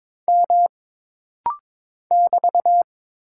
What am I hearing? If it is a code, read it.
Morse code, heard as a keyed tone, about 22 wpm, 700 Hz. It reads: M <BT>
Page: This rapid-fire timing is geared to push your ICR (Instant Character Recognition) skills.